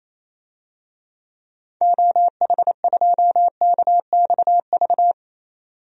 Morse code, heard as a keyed tone, about 28 wpm, 700 Hz. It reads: OH2KXV
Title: DXCC-100 Call Signs - 6 Characters - 28wpm